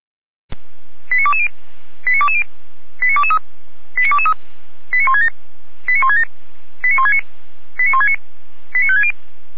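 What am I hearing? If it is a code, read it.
Five-tone selective call (ZVEI), heard as quick runs of five tones, tones of about 70 ms. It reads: 89220 89220 89223 89223 89167 89167 89169 89169 89560
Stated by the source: ZVEI five-tone call received by SDR, from the feuerwehr-orlamuende.org audio server.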